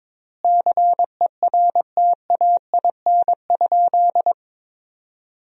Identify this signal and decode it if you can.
Morse code, heard as a keyed tone, about 22 wpm, 700 Hz. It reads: CERTAIN?
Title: 3x Word-Spacing - Sentences from Top 500 Words - 22wpm